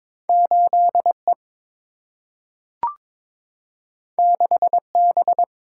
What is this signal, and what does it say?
Morse code, heard as a keyed tone, about 22 wpm, 700 Hz. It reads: 8 E 6 B
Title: Single-Letter Single-Number 22wpm